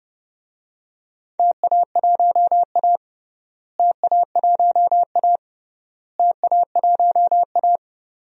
Morse code, heard as a keyed tone, about 31 wpm, 700 Hz. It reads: TA1A TA1A TA1A
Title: DXCC-100 Call Signs - 4 Characters - Repeated 3x - 30wpm